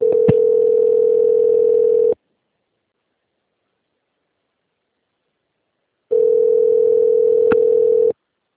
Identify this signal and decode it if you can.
Ringback tone (North America), 440 + 480 Hz.